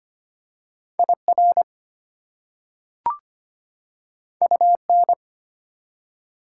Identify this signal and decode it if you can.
Morse code, heard as a keyed tone, about 25 wpm, 700 Hz. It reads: IR UN
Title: Prefixes and Suffixes 25wpm